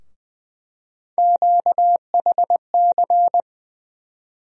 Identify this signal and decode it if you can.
Morse code, heard as a keyed tone, about 20 wpm, 700 Hz. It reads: QHC
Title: USA Call Sign Postfixes - 3 Letters 20wpm